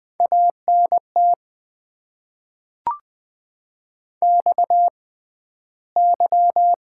Morse code, heard as a keyed tone, about 20 wpm, spaced as for 12 wpm, 700 Hz. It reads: ANT XY
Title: Top 90 QSO Elements - 6x Character Spacing - 20wpm